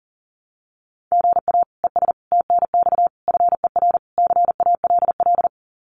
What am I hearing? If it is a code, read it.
Morse code, heard as a keyed tone, about 38 wpm, 700 Hz. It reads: GA ES TNX FER CALL
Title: Common QSO Phrases Encore 40wpm